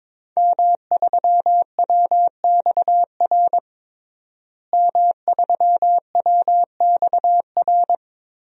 Morse code, heard as a keyed tone, about 22 wpm, 700 Hz. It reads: M3WXR M3WXR
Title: DXCC-100 Call Signs - 5 Characters - Repeated 3x - 22wpm